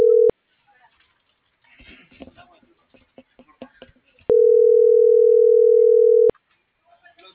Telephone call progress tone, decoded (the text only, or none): ringback tone